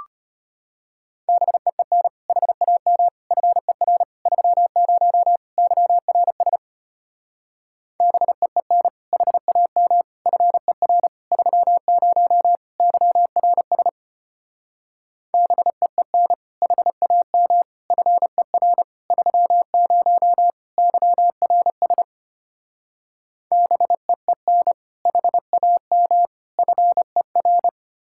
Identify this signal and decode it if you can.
Morse code, heard as a keyed tone, about 33 wpm, 700 Hz. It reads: BEEN HAM FER 30 YRS BEEN HAM FER 30 YRS BEEN HAM FER 30 YRS BEEN HAM FER